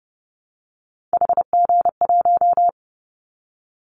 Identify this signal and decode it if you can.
Morse code, heard as a keyed tone, about 30 wpm, 700 Hz. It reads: HG1
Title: Callsign Prefixes for Top 25 DXCC - 30wpm